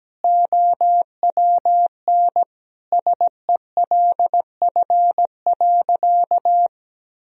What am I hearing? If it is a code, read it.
Morse code, heard as a keyed tone, about 17 wpm, 700 Hz. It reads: OWN SELF.
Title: Sentences from Top 1000 Words - Review Part 1 - 17wpm